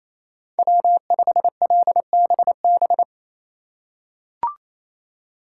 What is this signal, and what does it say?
Morse code, heard as a keyed tone, about 28 wpm, 700 Hz. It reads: W5LBB
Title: USA Call Signs - 5 Characters - 2x Character Spacing - 28wpm